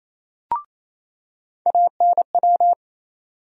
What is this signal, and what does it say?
Morse code, heard as a keyed tone, about 28 wpm, 700 Hz. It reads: ANW
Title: USA Call Sign Postfixes - 3 Letters 28wpm